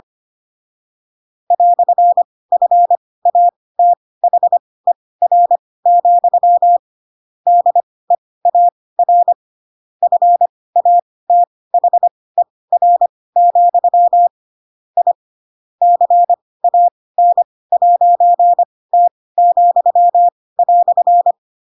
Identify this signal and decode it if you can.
Morse code, heard as a keyed tone, about 25 wpm, spaced as for 18 wpm, 700 Hz. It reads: "FATHER, DEAR FATHER, I CAN'T,"
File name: Pinocchio / Pinocchio ch7 12 25 ik1hge